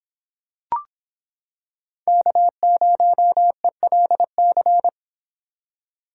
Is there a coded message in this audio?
Yes: Morse code from 2.1 to 4.9 s, heard as a keyed tone, about 26 wpm, 700 Hz. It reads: K0ELC